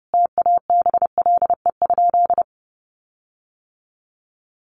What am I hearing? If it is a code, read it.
Morse code, heard as a keyed tone, about 30 wpm, 700 Hz. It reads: TABLE?
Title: Sentences from Top 1500 Words 30wpm